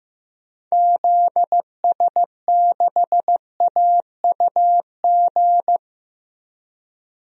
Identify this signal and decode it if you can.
Morse code, heard as a keyed tone, about 16 wpm, 700 Hz. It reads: ZS6AUG
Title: DXCC-100 Call Signs - 6 Characters - Repeated 2x - 15wpm